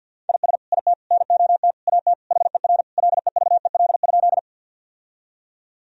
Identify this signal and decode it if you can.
Morse code, heard as a keyed tone, about 50 wpm, 700 Hz. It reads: IS IT NOT AT HER LEVEL?
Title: Sentences from Top 1000 Words - Review Part 4 - 50wpm